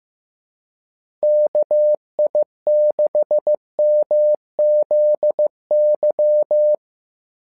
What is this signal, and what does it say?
Morse code, heard as a keyed tone, about 15 wpm, 600 Hz. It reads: KI6MZY
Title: SOTA QSO Practice 15wpm